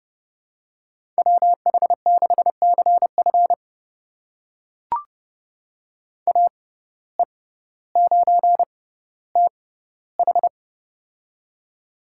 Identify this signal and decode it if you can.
Morse code, heard as a keyed tone, about 31 wpm, spaced as for 15 wpm, 700 Hz. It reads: WH6CF AE9TH
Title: USA Call Signs - 5 Characters - 6x Character Spacing - 30wpm